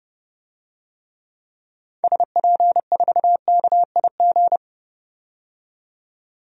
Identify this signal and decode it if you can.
Morse code, heard as a keyed tone, about 31 wpm, 700 Hz. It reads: SP4KIG
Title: DXCC-100 Call Signs - 6 Characters - 30wpm